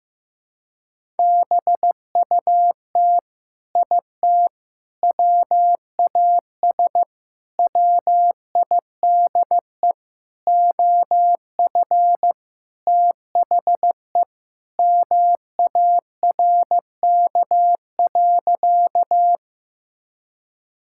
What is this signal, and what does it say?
Morse code, heard as a keyed tone, about 15 wpm, 700 Hz. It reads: BUT IT WAS WIDE OF THE MARK.
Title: Sentences from Top 1000 Words - Review Part 6 - 15wpm